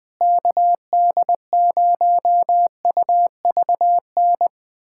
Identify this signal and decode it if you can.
Morse code, heard as a keyed tone, about 20 wpm, 700 Hz. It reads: KD0UVN
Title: USA Callsign DE Callsign 20wpm